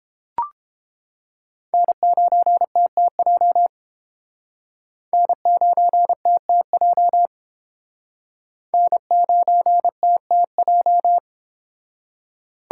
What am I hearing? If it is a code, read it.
Morse code, heard as a keyed tone, about 30 wpm, 700 Hz. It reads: N9TTJ N9TTJ N9TTJ